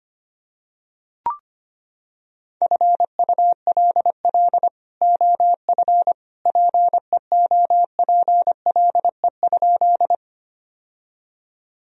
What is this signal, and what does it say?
Morse code, heard as a keyed tone, about 25 wpm, 700 Hz. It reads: FULL OF PEOPLE?
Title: Sentences from Top 1000 Words - Review Part 3 - 25wpm